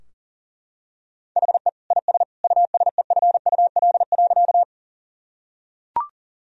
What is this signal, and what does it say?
Morse code, heard as a keyed tone, about 40 wpm, 700 Hz. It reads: HE IS USEFUL.
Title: Sentences from Top 1000 Words - Review Part 4 - 40wpm